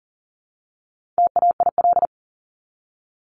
Morse code, heard as a keyed tone, about 40 wpm, 700 Hz. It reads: TAIL
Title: Top 500 Words 40wpm